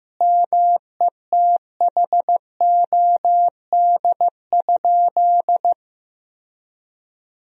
Morse code, heard as a keyed tone, about 16 wpm, 700 Hz. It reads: METHOD?
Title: Sentences from Top 1000 Words - Review Part 3 - 15wpm